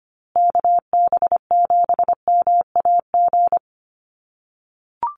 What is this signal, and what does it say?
Morse code, heard as a keyed tone, about 25 wpm, 700 Hz. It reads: KB7MAG
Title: USA Call Signs Repeated 3x 25wpm